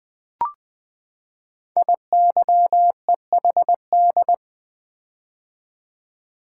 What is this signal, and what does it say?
Morse code, heard as a keyed tone, about 20 wpm, 700 Hz. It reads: IYEHD
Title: Code Groups - 5 Letters - 20wpm